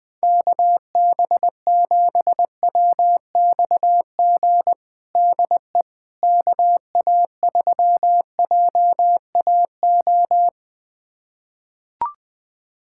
Morse code, heard as a keyed tone, about 21 wpm, 700 Hz. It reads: KB7WXG DE KA3JAO
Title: USA Callsign DE Callsign 20wpm